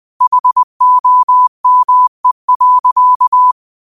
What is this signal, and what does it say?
Morse code, heard as a keyed tone, about 20 wpm, 1000 Hz. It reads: HOME.
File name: qotdfiles / 20wpm